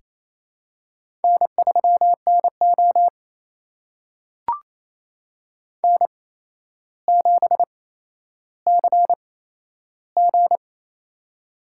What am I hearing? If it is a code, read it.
Morse code, heard as a keyed tone, about 28 wpm, spaced as for 12 wpm, 700 Hz. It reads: N3NO N7CG
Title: USA Call Signs - 4 Characters - 8x Character Spacing - 28wpm